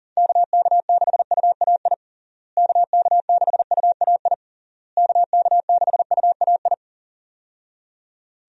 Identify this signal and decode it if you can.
Morse code, heard as a keyed tone, about 40 wpm, 700 Hz. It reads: KK6UAI KK6UAI KK6UAI
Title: USA Call Signs - 6 Characters - Repeated 3x - 40wpm